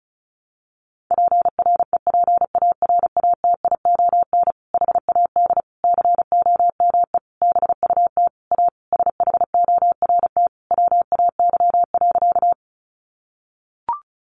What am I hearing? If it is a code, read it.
Morse code, heard as a keyed tone, about 34 wpm, 700 Hz. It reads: PREPARATION HAD COME BUT A SHORT WAY.
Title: Sentences from Top 1400 Words 35wpm